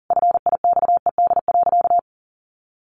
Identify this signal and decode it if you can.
Morse code, heard as a keyed tone, about 40 wpm, 700 Hz. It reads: FIXED.
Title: Sentences from Top 1100 Words 40wpm